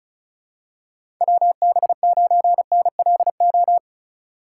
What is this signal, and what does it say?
Morse code, heard as a keyed tone, about 34 wpm, 700 Hz. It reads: WB9NLO
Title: USA Call Signs - 4x Character Spacing - 35wpm